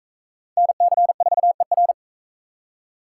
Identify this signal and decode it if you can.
Morse code, heard as a keyed tone, about 41 wpm, 700 Hz. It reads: NC4ER